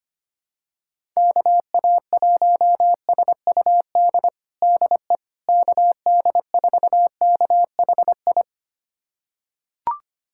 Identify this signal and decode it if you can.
Morse code, heard as a keyed tone, about 25 wpm, 700 Hz. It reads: KA1SUD DE KD4KHI
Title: USA Callsign DE Callsign - 6 Characters - 25wpm